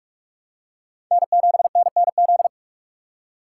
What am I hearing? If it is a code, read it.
Morse code, heard as a keyed tone, about 43 wpm, 700 Hz. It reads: N7NNZ
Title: USA Call Signs - 5 Characters - 2x Character Spacing - 45wpm